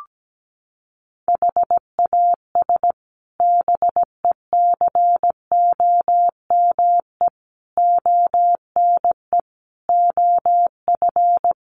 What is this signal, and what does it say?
Morse code, heard as a keyed tone, about 17 wpm, 700 Hz. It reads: HAS BECOME ONE OF
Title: Sets of 5 Words Encore 17wpm